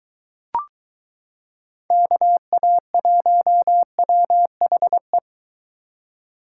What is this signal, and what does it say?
Morse code, heard as a keyed tone, about 22 wpm, 700 Hz. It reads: KA1WHE